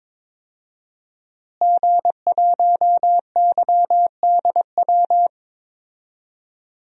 Morse code, heard as a keyed tone, about 22 wpm, 700 Hz. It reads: G1YDW
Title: DXCC-100 Call Signs - 5 Characters - Repeated 2x - 22wpm